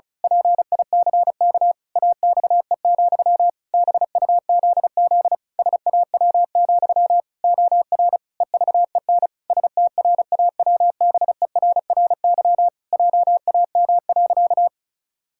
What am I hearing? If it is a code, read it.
Morse code, heard as a keyed tone, about 36 wpm, 700 Hz. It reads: PICK AXE, BUZZ SAW, OR EVEN STRAWBERRY JAM.
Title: Pangram Challenge v2 - 35wpm